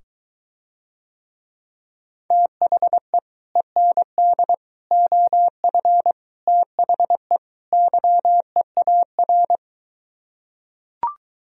Morse code, heard as a keyed tone, about 23 wpm, 700 Hz. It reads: THE END OF THE YEAR